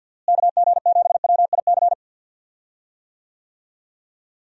Morse code, heard as a keyed tone, about 45 wpm, 700 Hz. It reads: KG7WIC
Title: USA Call Signs - 6 Characters - Repeated 2x - 50wpm